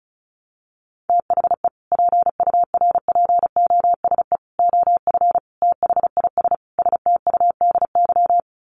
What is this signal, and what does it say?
Morse code, heard as a keyed tone, about 34 wpm, 700 Hz. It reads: THE PURPOSE OF THIS STUDY
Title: Sets of 5 Words 35wpm